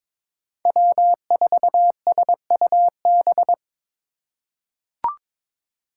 Morse code, heard as a keyed tone, about 23 wpm, 700 Hz. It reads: W4SUB
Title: USA Call Signs - 5 Characters - 3x Character Spacing - 22wpm